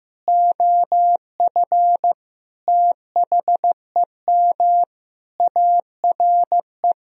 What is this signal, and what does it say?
Morse code, heard as a keyed tone, about 15 wpm, 700 Hz. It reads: OF THEM ARE
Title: Sentences from Top 1000 Words - Review Part 1 - 15wpm